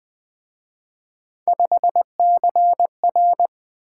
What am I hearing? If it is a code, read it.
Morse code, heard as a keyed tone, about 24 wpm, 700 Hz. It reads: 5CR
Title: Speed-Racing - Three Letter-Number Combinations - 20wpm